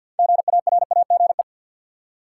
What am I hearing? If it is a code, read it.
Morse code, heard as a keyed tone, about 52 wpm, 700 Hz. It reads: GARAGE
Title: Six Letter Words - 50wpm